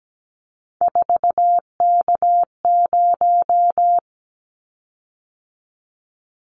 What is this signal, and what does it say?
Morse code, heard as a keyed tone, about 17 wpm, 700 Hz. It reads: 4K0
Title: Callsign Prefixes for Top 100 DXCC - 17wpm